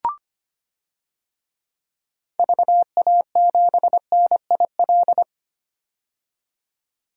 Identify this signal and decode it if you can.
Morse code, heard as a keyed tone, about 25 wpm, 700 Hz. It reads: VA7NIL
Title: DXCC-100 Call Signs - 6 Characters - 25wpm